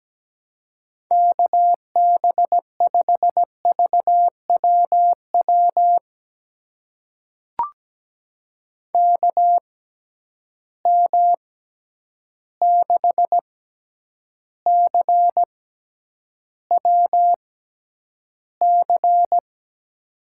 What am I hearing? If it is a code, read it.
Morse code, heard as a keyed tone, about 17 wpm, spaced as for 9 wpm, 700 Hz. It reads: KB5VWW KM6CWC
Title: USA Call Signs - 6 Characters - 6x Character Spacing - 17wpm